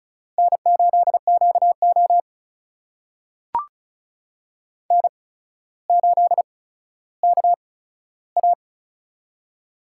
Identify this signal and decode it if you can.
Morse code, heard as a keyed tone, about 36 wpm, spaced as for 15 wpm, 700 Hz. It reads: N8QO N8KA
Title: USA Call Signs - 4 Characters - 8x Character Spacing - 35wpm